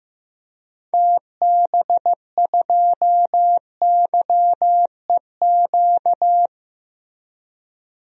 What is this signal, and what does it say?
Morse code, heard as a keyed tone, about 16 wpm, 700 Hz. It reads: TB2YEQ
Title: DXCC-100 Call Signs - 6 Characters - Repeated 2x - 15wpm